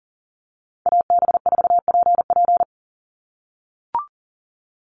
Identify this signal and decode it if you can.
Morse code, heard as a keyed tone, about 40 wpm, 700 Hz. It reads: AB4PP
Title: NA SOTA Chaser Callsigns 40wpm